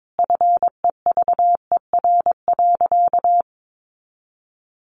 Morse code, heard as a keyed tone, about 22 wpm, 700 Hz. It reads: FEVER.
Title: A Sound Of Thunder 22wpm